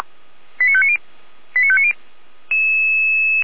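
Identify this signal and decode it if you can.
Five-tone selective call (ZVEI), heard as quick runs of five tones, tones of about 70 ms. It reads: 89590 89590
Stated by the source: ZVEI five-tone call received by SDR.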